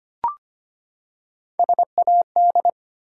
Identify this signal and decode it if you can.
Morse code, heard as a keyed tone, about 25 wpm, 700 Hz. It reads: SAD
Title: Three Letter Words - 25wpm